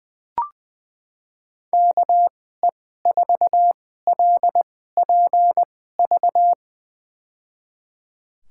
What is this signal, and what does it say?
Morse code, heard as a keyed tone, about 21 wpm, spaced as for 15 wpm, 700 Hz. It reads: KE4LPV
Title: USA Call Signs - 6 Characters - 2x Character Spacing - 20wpm